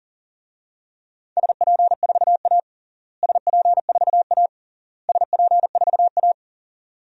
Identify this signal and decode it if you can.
Morse code, heard as a keyed tone, about 40 wpm, 700 Hz. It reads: SP4A SP4A SP4A
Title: DXCC-100 Call Signs - 4 Characters - Repeated 3x - 40wpm